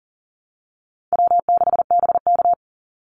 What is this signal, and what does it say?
Morse code, heard as a keyed tone, about 40 wpm, 700 Hz. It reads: W6BK
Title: Easy Call Signs Repeated 3x - 40wpm